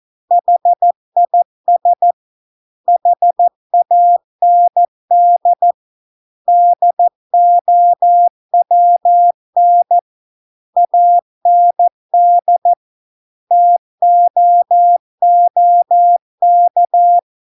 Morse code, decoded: HIS HAND DOWN AND TOOK

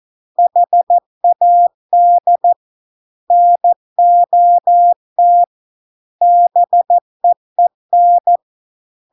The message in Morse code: HAD NOT BEEN

